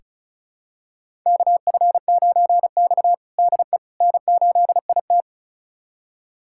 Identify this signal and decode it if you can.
Morse code, heard as a keyed tone, about 34 wpm, 700 Hz. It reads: KF9X DE N8IT
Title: USA Callsign DE Callsign - 4 Characters - 35wpm